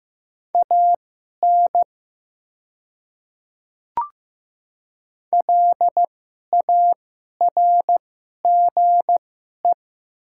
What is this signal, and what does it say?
Morse code, heard as a keyed tone, about 15 wpm, spaced as for 11 wpm, 700 Hz. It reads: AN LARGE
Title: 2x Character Spacing - Top 200 Words - 15wpm